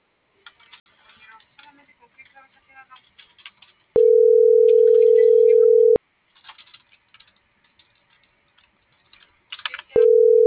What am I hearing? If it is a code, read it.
Ringback tone (North America), 440 + 480 Hz.